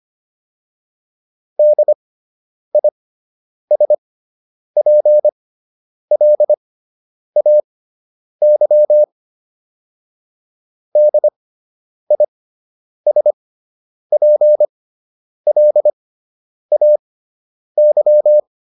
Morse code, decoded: DISPLAY DISPLAY

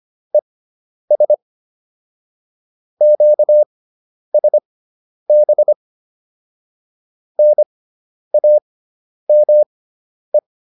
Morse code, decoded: ES QSB NAME

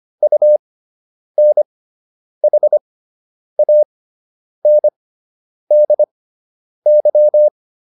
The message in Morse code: UNHANDY

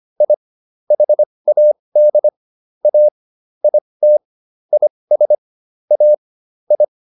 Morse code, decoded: I HAD A IT IS A I

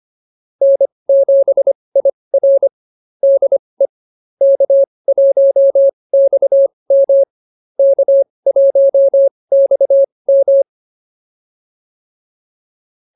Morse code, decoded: N7IR DE K1XM K1XM

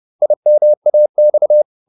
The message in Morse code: IMAX

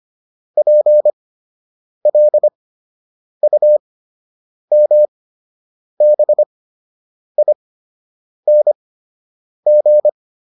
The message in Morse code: PLUMBING

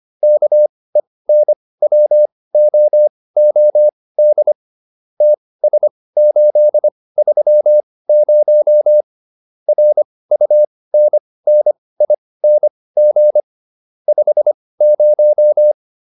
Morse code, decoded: KENWOOD TS830 RUNNING 50

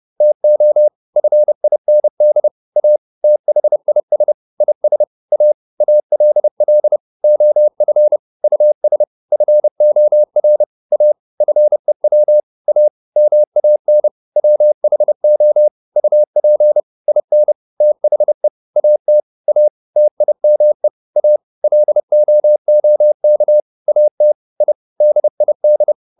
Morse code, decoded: TO FIND A THIS IS A ALL OF US FOR A FEW A MAN WHO UP IN THE AT A TIME A LOOK AT I DID

